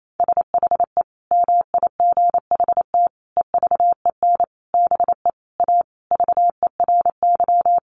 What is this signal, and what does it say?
Morse code, heard as a keyed tone, about 28 wpm, 700 Hz. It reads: SHE MIGHT EVEN BE A VERY
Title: Sentences from Top 1500 Words 28wpm